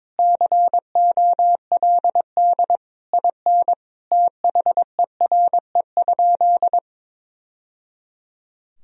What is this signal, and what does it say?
Morse code, heard as a keyed tone, about 22 wpm, 700 Hz. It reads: COLD IN THERE?
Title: Sentences from Top 1000 Words - Review Part 4 - 22wpm